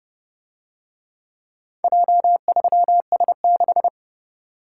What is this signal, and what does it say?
Morse code, heard as a keyed tone, about 29 wpm, 700 Hz. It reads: J3S6